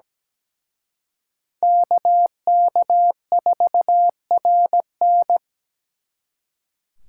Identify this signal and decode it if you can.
Morse code, heard as a keyed tone, about 20 wpm, 700 Hz. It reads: KK4RN
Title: Speed Racing - USA 2x2 Call Signs 17wpm